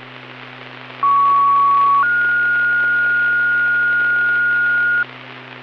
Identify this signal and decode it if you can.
Two-tone page, heard as a one-second tone, then a three-second tone, about 1120 then 1430 Hz, with noise. It is heard from 1.0 to 5.0 s.